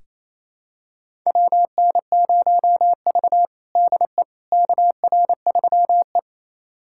Morse code, heard as a keyed tone, about 28 wpm, 700 Hz. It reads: WN0V DE KR3E
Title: USA Callsign DE Callsign - 4 Characters - 28wpm